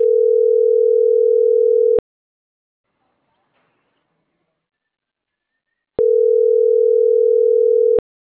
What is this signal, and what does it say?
Ringback tone (North America), 440 + 480 Hz.